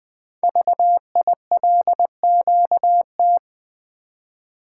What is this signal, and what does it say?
Morse code, heard as a keyed tone, about 20 wpm, 700 Hz. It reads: VILQT
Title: Code Groups - 5 Letters - 20wpm